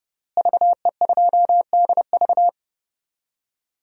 Morse code, heard as a keyed tone, about 30 wpm, 700 Hz. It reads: VE2DV
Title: DXCC-100 Call Signs - 5 Characters - Repeated 2x - 30wpm